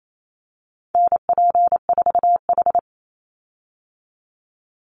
Morse code, heard as a keyed tone, about 28 wpm, 700 Hz. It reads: NP4H
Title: Call Signs Repeated 2x - 28wpm